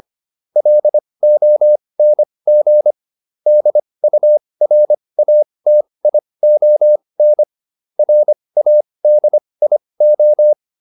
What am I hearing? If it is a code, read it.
Morse code, heard as a keyed tone, about 25 wpm, spaced as for 20 wpm, 600 Hz. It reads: LONG DURATION RADIO